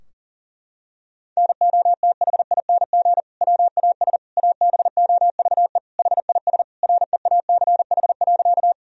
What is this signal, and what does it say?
Morse code, heard as a keyed tone, about 37 wpm, 700 Hz. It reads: NOTHING WAS ABOVE HIS REACH.
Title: Sentences from Top 1000 Words - Review Part 2 - 40wpm